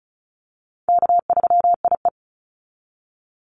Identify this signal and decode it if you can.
Morse code, heard as a keyed tone, about 34 wpm, 700 Hz. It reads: K3IE
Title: Easy Call Signs Repeated 2x - 35wpm